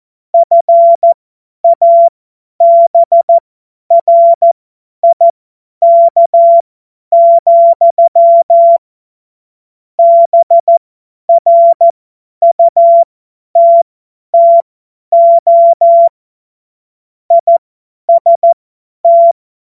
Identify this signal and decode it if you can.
Morse code, heard as a keyed tone, about 14 wpm, spaced as for 10 wpm, 680 Hz. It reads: FABRIK, BRUTTO IST
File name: Morsekurs / Witze / Witz 19. April 2022 50 F70